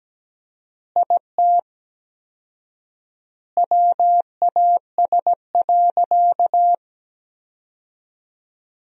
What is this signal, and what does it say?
Morse code, heard as a keyed tone, about 17 wpm, 700 Hz. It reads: IT WAS.
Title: 4x Word-Spacing - Sentences from Top 100 Words - 17wpm